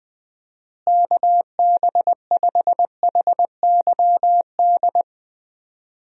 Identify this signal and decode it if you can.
Morse code, heard as a keyed tone, about 20 wpm, 700 Hz. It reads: KB5HYD
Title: USA Call Signs - 6 Characters - Repeated 2x - 20wpm